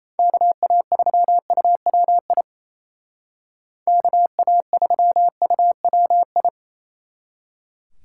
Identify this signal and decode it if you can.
Morse code, heard as a keyed tone, about 34 wpm, 700 Hz. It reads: KA3UWI KA3UWI
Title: Speed Racing - USA 2x3 Call Signs 28wpm